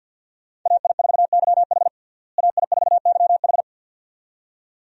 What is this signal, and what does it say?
Morse code, heard as a keyed tone, about 50 wpm, 700 Hz. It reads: AI4YH AI4YH
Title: USA Call Signs - 5 Characters - Repeated 2x - 50wpm